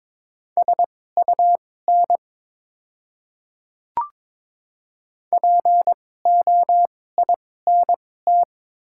Morse code, heard as a keyed tone, about 22 wpm, spaced as for 16 wpm, 700 Hz. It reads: SUN POINT
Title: 2x Character Spacing - Top 300 Words - 22wpm